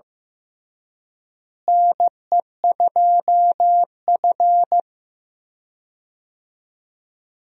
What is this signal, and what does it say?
Morse code, heard as a keyed tone, about 16 wpm, 700 Hz. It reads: NE2F
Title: USA Call Signs - 4 Characters - Repeated 3x - 15wpm